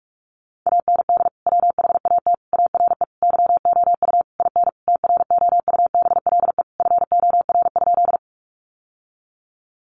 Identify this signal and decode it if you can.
Morse code, heard as a keyed tone, about 44 wpm, 700 Hz. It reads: AND WHAT ARE YOU IN TROUBLE FOR?
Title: Sentences from Top 1200 Words 45wpm